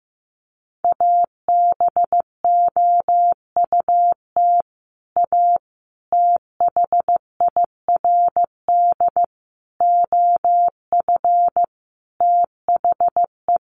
Morse code, heard as a keyed tone, about 15 wpm, 700 Hz. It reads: ABOUT A THIRD OF THE
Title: Sets of 5 Words Encore2 15wpm